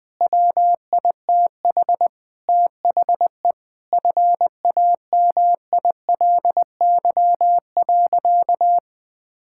Morse code, heard as a keyed tone, about 20 wpm, 700 Hz. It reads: WITH THE FAMILY.
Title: Sentences with 7-Word Phrases - 20wpm